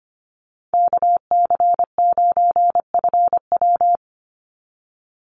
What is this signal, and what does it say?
Morse code, heard as a keyed tone, about 25 wpm, 700 Hz. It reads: KC9FW
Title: USA Call Signs 25wpm